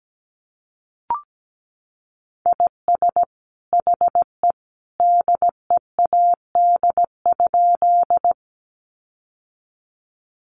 Morse code, heard as a keyed tone, about 17 wpm, 700 Hz. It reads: IS HE DEAD?
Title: Sentences from Top 1500 Words 17wpm